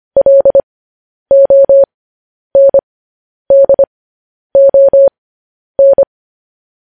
Morse code, heard as a keyed tone, about 25 wpm, spaced as for 10 wpm, 550 Hz. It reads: LONDON
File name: podcast / News headlines-1x-25x10